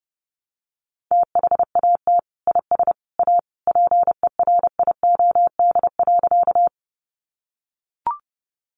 Morse code, heard as a keyed tone, about 30 wpm, 700 Hz. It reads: THAT IS A PERIOD.
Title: Sentences from Top 800 Words 30wpm